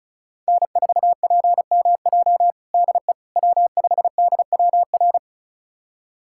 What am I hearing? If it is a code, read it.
Morse code, heard as a keyed tone, about 36 wpm, 700 Hz. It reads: N4PMJ DE W5DWR
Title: USA Callsign DE Callsign - 5 Characters - 35wpm